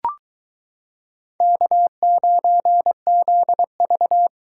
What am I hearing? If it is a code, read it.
Morse code, heard as a keyed tone, about 22 wpm, 700 Hz. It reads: K9ZV